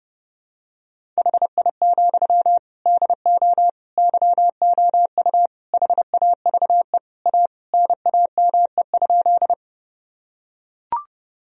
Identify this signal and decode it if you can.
Morse code, heard as a keyed tone, about 32 wpm, 700 Hz. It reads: HI, DO YOU HAVE A NAME?
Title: Sentences from Top 1000 Words - Review Part 3 - 30wpm